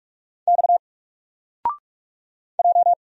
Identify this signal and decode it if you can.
Morse code, heard as a keyed tone, about 44 wpm, 700 Hz. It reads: X J
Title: Single Character (Letter:Number:Punctuation:Prosign) 45wpm